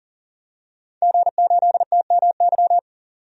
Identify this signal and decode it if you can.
Morse code, heard as a keyed tone, about 40 wpm, 700 Hz. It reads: G8TMY
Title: DXCC-100 Call Signs - 5 Characters - 40wpm